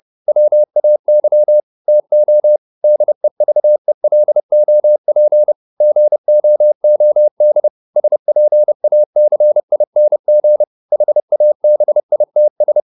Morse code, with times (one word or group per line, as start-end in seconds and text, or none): 0.3-1.6 WAY
1.9-2.6 TO
2.8-5.5 DEVELOP
5.8-7.7 GOOD
8.0-10.6 SPACING
10.9-12.8 HABITS